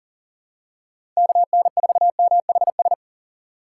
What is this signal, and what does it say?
Morse code, heard as a keyed tone, about 40 wpm, 700 Hz. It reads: KN4MHS
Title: USA Call Signs - 6 Characters - 3x Character Spacing - 40wpm